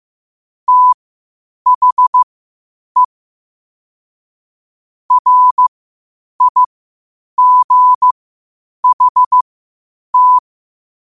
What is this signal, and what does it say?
Morse code, heard as a keyed tone, about 15 wpm, spaced as for 8 wpm, 1000 Hz. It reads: THE RIGHT
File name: qotdfiles / 8wpm